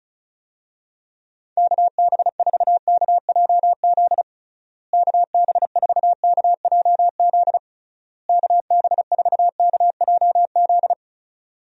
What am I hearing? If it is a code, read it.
Morse code, heard as a keyed tone, about 34 wpm, 700 Hz. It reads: KB4KJZ KB4KJZ KB4KJZ
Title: USA Call Signs - 6 Characters - Repeated 3x - 35wpm